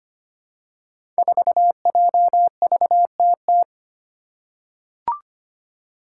Morse code, heard as a keyed tone, about 25 wpm, 700 Hz. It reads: 4JVTT
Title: Five Challenging Letter-Number Combinations 25wpm